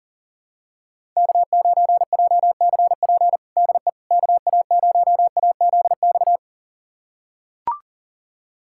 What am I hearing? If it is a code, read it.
Morse code, heard as a keyed tone, about 37 wpm, 700 Hz. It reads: K9JCP DE KA0AZX
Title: USA Callsign DE Callsign 40wpm